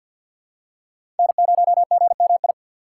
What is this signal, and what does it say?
Morse code, heard as a keyed tone, about 46 wpm, 700 Hz. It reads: N0GMI
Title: USA Call Signs - 6x Character Spacing - 50wpm